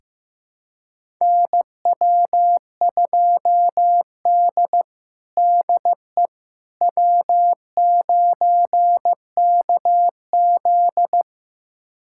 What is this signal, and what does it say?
Morse code, heard as a keyed tone, about 15 wpm, 700 Hz. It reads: NW2D DE W9KZ
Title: USA Callsign DE Callsign - 4 Characters - 15wpm